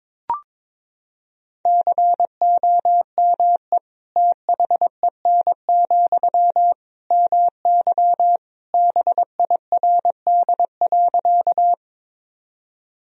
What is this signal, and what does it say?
Morse code, heard as a keyed tone, about 22 wpm, 700 Hz. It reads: COME THEN, MY BIRD.
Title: Sentences from Top 1000 Words - Review Part 2 - 22wpm